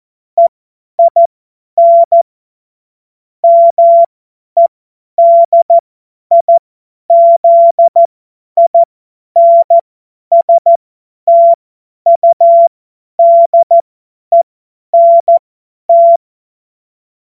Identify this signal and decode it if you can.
Morse code, heard as a keyed tone, about 14 wpm, spaced as for 10 wpm, 680 Hz. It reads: EIN MEDIZINSTUDENT
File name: Morsekurs / Witze / Witz 14. Juni 2022 50 F70